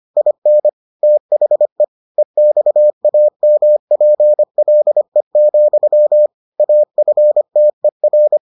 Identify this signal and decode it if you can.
Morse code, heard as a keyed tone, about 25 wpm, 600 Hz. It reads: IN THE EXAMPLE, AFTER